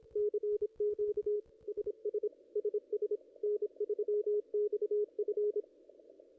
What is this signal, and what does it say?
Morse code, heard as a keyed tone, about 28 wpm, 410 Hz, with radio noise. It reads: CQ SS SS N3XF
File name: contests / ss-cw-k5tr-op-n5rz -0100